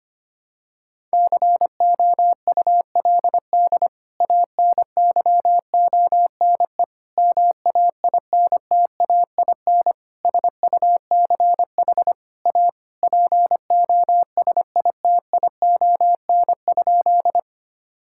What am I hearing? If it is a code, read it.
Morse code, heard as a keyed tone, about 25 wpm, 700 Hz. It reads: COULD ANYONE MAINTAIN SUCH A POSITION?
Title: Sentences from Top 1000 Words - Review Part 5 - 25wpm